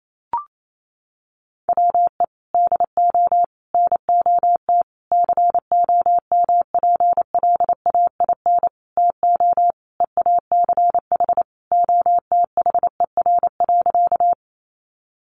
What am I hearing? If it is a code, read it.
Morse code, heard as a keyed tone, about 28 wpm, 700 Hz. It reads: WE DO NOT COMPLAIN TO EACH OTHER.
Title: Sentences from Top 1300 Words 28wpm